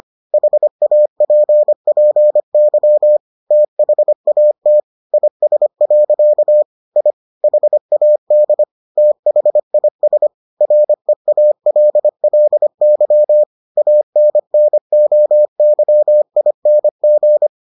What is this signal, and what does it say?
Morse code, heard as a keyed tone, about 25 wpm, 600 Hz. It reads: HAPPY THAT IS. I HAD THIS REALLY ANNOYING